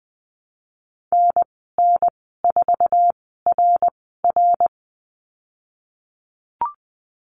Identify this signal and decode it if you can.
Morse code, heard as a keyed tone, about 20 wpm, spaced as for 14 wpm, 700 Hz. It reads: NN4RR
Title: 062 Lesson 11g - Letter h - Call Signs 20wpm